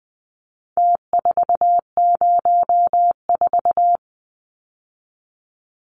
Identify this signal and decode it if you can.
Morse code, heard as a keyed tone, about 20 wpm, 700 Hz. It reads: T404
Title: Numbers - Contest Style - 4-Digit 20wpm